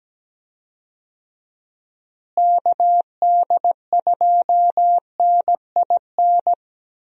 Morse code, heard as a keyed tone, about 17 wpm, 700 Hz. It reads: KD2NIN
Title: USA Call Signs - 6 Characters - Repeated 3x - 17wpm